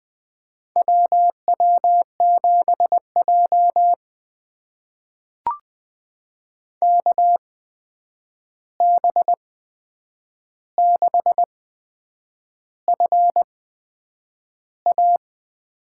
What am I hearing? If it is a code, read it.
Morse code, heard as a keyed tone, about 20 wpm, spaced as for 8 wpm, 700 Hz. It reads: WW7J KB6FA
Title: USA Call Signs - 8x Character Spacing - 20wpm